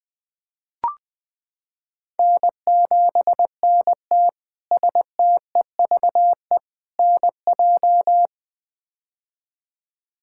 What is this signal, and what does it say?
Morse code, heard as a keyed tone, about 21 wpm, 700 Hz. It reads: N7NT STEVE NJ
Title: K1USN SST QSO Practice - 20wpm